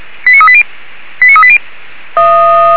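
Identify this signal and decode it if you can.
Five-tone selective call (ZVEI), heard as quick runs of five tones, tones of about 70 ms. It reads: 89380 89380
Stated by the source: ZVEI five-tone call 89401 received by SDR, archived by feuerwehr-orlamuende (Orlamünde fire brigade).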